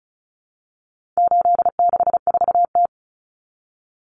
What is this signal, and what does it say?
Morse code, heard as a keyed tone, about 34 wpm, 700 Hz. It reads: 864T
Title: Common Cut Numbers - 4-Digit 35wpm